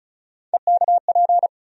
Morse code, heard as a keyed tone, about 34 wpm, 700 Hz. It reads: EKP
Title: USA Call Sign Postfixes - 3 Letters 35wpm